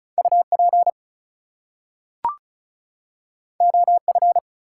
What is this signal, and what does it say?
Morse code, heard as a keyed tone, about 36 wpm, 700 Hz. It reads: UP OF
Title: Two Letter Words - 35wpm